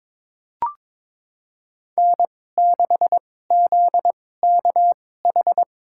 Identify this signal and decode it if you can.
Morse code, heard as a keyed tone, about 22 wpm, spaced as for 16 wpm, 700 Hz. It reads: N6ZKH
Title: USA Call Signs - 5 Characters - 2x Character Spacing - 22wpm